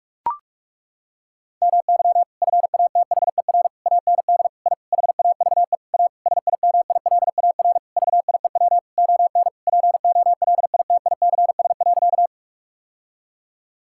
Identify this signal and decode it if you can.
Morse code, heard as a keyed tone, about 41 wpm, 700 Hz. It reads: MY FATHER AND I HAVE A SIMILAR VIEW ON POLITICS.
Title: Sentences from Top 1000 Words - Review Part 6 - 45wpm